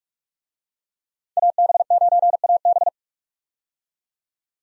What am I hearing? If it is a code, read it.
Morse code, heard as a keyed tone, about 41 wpm, 700 Hz. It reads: AB9AB
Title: USA Call Signs - 5 Characters - 45wpm